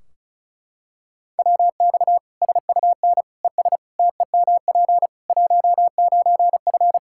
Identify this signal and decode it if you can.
Morse code, heard as a keyed tone, about 36 wpm, 700 Hz. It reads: WX SUN ES TEMP 19F
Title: Practice QSO - Exchange 2 - Info - 35wpm